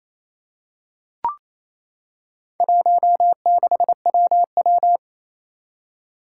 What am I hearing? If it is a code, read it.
Morse code, heard as a keyed tone, about 27 wpm, 700 Hz. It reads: 16WW
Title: Four Challenging Letter-Number Combinations 28wpm